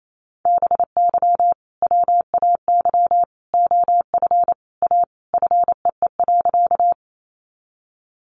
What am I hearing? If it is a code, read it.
Morse code, heard as a keyed tone, about 28 wpm, 700 Hz. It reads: BY WAY OF A FEE.
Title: Sentences from Top 1500 Words 28wpm